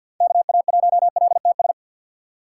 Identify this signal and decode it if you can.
Morse code, heard as a keyed tone, about 48 wpm, 700 Hz. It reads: KA1LTS
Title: USA Call Signs - 4x Character Spacing - 50wpm